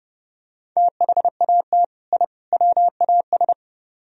Morse code, heard as a keyed tone, about 29 wpm, 700 Hz. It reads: THAT I WAS